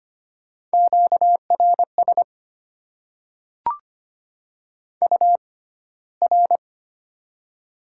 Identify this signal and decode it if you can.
Morse code, heard as a keyed tone, about 25 wpm, spaced as for 15 wpm, 690 Hz. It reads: QRS UR
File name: Top 90 QSO Elements - 6x Character Spacing - 25wpm